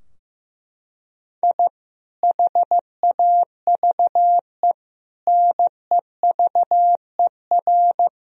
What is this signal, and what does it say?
Morse code, heard as a keyed tone, about 15 wpm, 700 Hz. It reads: I HAVE NEVER
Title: Sentences from Top 1000 Words - Review Part 3 - 15wpm